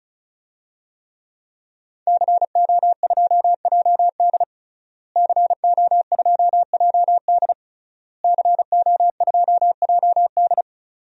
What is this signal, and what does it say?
Morse code, heard as a keyed tone, about 34 wpm, 700 Hz. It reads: CO2JD CO2JD CO2JD
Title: DXCC-100 Call Signs - 5 Characters - Repeated 3x - 35wpm